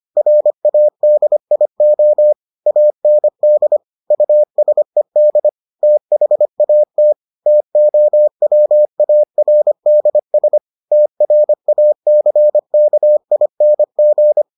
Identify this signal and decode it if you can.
Morse code, heard as a keyed tone, about 25 wpm, 600 Hz. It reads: RADIO AND USED THAT TOWARDS TRACKING